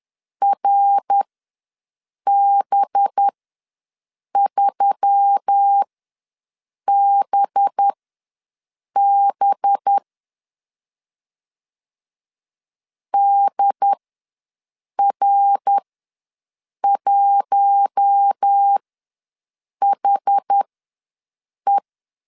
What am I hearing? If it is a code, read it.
Morse code, heard as a keyed tone, about 10 wpm, spaced as for 6 wpm, 800 Hz. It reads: RB3BB DR1HE
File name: LEZIONI / 5 LBVC / 5 LB ws 10wpm 3616